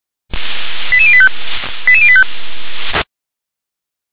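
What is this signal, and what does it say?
Five-tone selective call (ZVEI), heard as quick runs of five tones, tones of about 70 ms. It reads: 88085 88085
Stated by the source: ZVEI five-tone call 88851 received by SDR.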